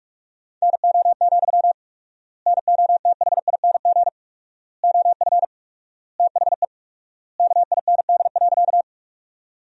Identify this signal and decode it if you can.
Morse code, heard as a keyed tone, about 47 wpm, 700 Hz. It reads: NO, NOTHING OF THE KIND.
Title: 4x Word-Spacing - Sentences from Top 500 Words - 45wpm